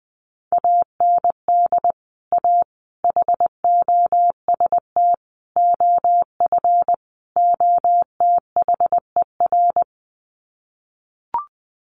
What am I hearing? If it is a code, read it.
Morse code, heard as a keyed tone, about 19 wpm, 700 Hz. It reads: AND A HOST OF OTHER
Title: Sets of 5 Words Encore2 20wpm